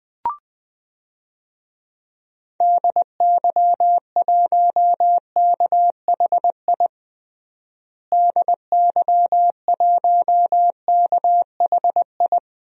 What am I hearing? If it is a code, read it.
Morse code, heard as a keyed tone, about 21 wpm, 700 Hz. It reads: DY1KHI DY1KHI
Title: DXCC-100 Call Signs - 6 Characters - Repeated 2x - 20wpm